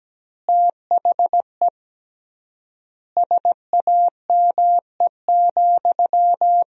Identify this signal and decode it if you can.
Morse code, heard as a keyed tone, about 17 wpm, 700 Hz. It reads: THE SAME,
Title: 3x Word-Spacing - Sentences from Top 300 Words - 17wpm